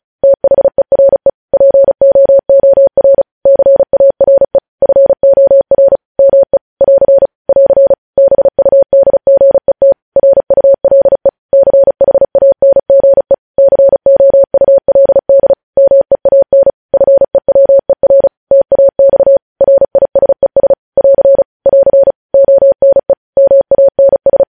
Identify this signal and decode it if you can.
Morse code, heard as a keyed tone, about 36 wpm, 550 Hz. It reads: THERE POOR CARE FOR ME <AR> <AR> BUDGET RULE CHANGE COULD MEAN FEWER TAX RISES <AR> <AR> ONE MANS